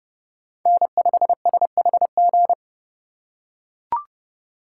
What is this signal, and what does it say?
Morse code, heard as a keyed tone, about 30 wpm, 700 Hz. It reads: N5SHG
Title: USA Call Signs - 5 Characters - 4x Character Spacing - 30wpm